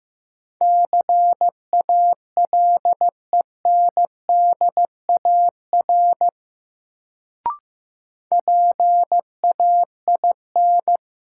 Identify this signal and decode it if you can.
Morse code, heard as a keyed tone, about 15 wpm, 700 Hz. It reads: CALENDAR PAIN
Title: Sentences from Top 1600 Words - New Word Drill - 15wpm